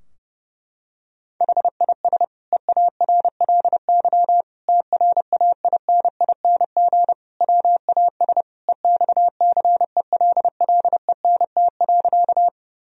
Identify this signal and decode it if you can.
Morse code, heard as a keyed tone, about 30 wpm, 700 Hz. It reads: HIS EARLY TRAINING WAS EXCELLENT.
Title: Sentences from Top 1000 Words - Review Part 4 - 30wpm